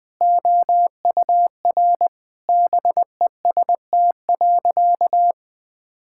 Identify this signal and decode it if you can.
Morse code, heard as a keyed tone, about 20 wpm, 700 Hz. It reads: OUR BEST.
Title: Sentences with 7-Word Phrases - 20wpm